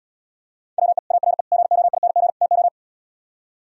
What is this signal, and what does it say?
Morse code, heard as a keyed tone, about 77 wpm, 700 Hz. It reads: WE ARE GOING TO